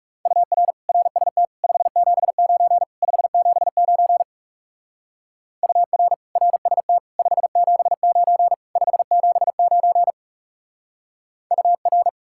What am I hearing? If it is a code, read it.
Morse code, heard as a keyed tone, about 35 wpm, 700 Hz. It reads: UR RST 579 579 UR RST 579 579 UR
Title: Speed-Racing Common QSO Phrases Encore - 30wpm